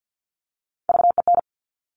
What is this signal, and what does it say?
Morse code, heard as a keyed tone, about 50 wpm, 700 Hz. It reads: VEN
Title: Top Three Letter Combinations in Words 50wpm